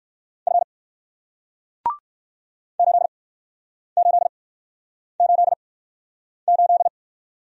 Morse code, heard as a keyed tone, about 57 wpm, spaced as for 9 wpm, 700 Hz. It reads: F 8888